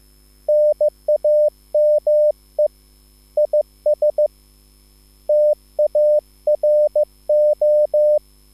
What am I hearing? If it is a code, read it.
Morse code, heard as a keyed tone, about 15 wpm, 600 Hz. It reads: NAME IS TARO
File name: QSL / CW lesson